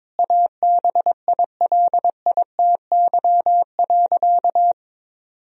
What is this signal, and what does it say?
Morse code, heard as a keyed tone, about 22 wpm, 700 Hz. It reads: ABILITY.
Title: Sentences from Top 1000 Words - Review Part 3 - 22wpm